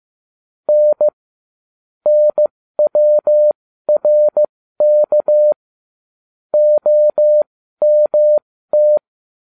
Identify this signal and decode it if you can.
Morse code, heard as a keyed tone, about 15 wpm, spaced as for 13 wpm, 600 Hz. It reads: N NWRK OMT